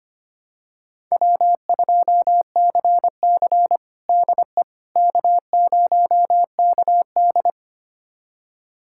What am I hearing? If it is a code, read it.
Morse code, heard as a keyed tone, about 25 wpm, 700 Hz. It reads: W2CC DE K0KD
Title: USA Callsign DE Callsign - 4 Characters - 25wpm